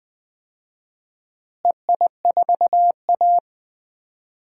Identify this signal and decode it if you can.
Morse code, heard as a keyed tone, about 20 wpm, 700 Hz. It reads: EI4A
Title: DXCC-100 Call Signs - 4 Characters - Repeated 3x - 20wpm